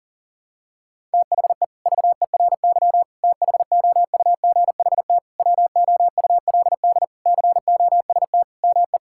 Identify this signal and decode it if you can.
Morse code, heard as a keyed tone, about 40 wpm, 700 Hz. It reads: THE VERY THOUGHT WOULD COST ME
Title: Sentences from Top 1000 Words - Review Part 5 - 40wpm